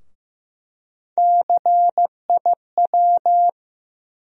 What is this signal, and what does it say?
Morse code, heard as a keyed tone, about 15 wpm, 700 Hz. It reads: CIW
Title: USA Call Sign Postfixes - 3 Letters 15wpm